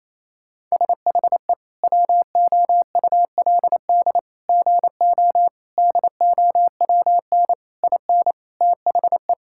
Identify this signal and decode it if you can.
Morse code, heard as a keyed tone, about 28 wpm, 700 Hz. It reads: SHE WOULD GO DOWN IN THE
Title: Sentences from Top 1600 Words 28wpm